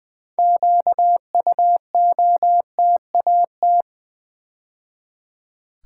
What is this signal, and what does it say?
Morse code, heard as a keyed tone, about 20 wpm, 700 Hz. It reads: QUOTAT
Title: Build A Word - 3 Letters At A Time - 9-Letter Words - 20wpm